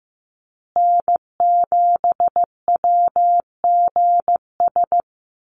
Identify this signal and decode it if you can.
Morse code, heard as a keyed tone, about 15 wpm, 700 Hz. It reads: N7WGS
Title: USA Call Signs Repeated 3x 15wpm